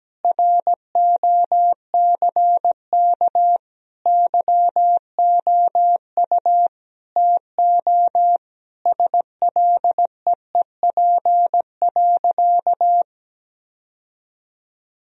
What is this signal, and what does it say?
Morse code, heard as a keyed tone, about 17 wpm, 700 Hz. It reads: ROCK YOU TO SLEEP.
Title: Sentences from Top 1000 Words - Review Part 2 - 17wpm